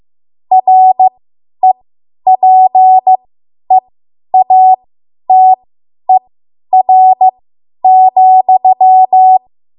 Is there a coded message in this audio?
Yes: Morse code from 0.5 to 9.4 s, heard as a keyed tone, about 15 wpm, spaced as for 10 wpm, 750 Hz. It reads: REPEATER,